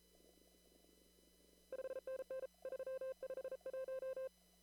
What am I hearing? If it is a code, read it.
Morse code, heard as a keyed tone, about 34 wpm, 520 Hz. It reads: 5NN 351
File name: CW / kt5j wpx cw-op-k5pi